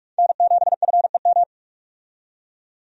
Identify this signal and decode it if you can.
Morse code, heard as a keyed tone, about 46 wpm, 700 Hz. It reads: N7FEM
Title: USA Call Signs - 5 Characters - 45wpm